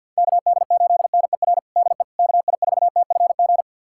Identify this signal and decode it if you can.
Morse code, heard as a keyed tone, about 47 wpm, 700 Hz. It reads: KD8NER DE KI4TFG
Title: USA Callsign DE Callsign - 6 Characters - 50wpm